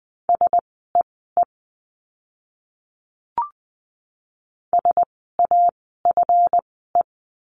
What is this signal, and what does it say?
Morse code, heard as a keyed tone, about 20 wpm, spaced as for 14 wpm, 700 Hz. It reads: SEE SAFE
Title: 139 Lesson 23f - Letter f - Words 20wpm